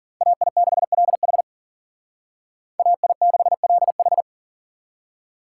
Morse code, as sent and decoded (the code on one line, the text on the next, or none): .- .. -.... .-.. .... / .- .. -.... .-.. ....
AI6LH AI6LH